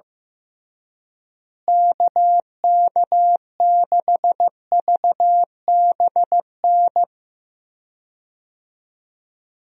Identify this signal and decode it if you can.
Morse code, heard as a keyed tone, about 15 wpm, 700 Hz. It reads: KK6VBN
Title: USA Call Signs - 6 Characters - Repeated 3x - 15wpm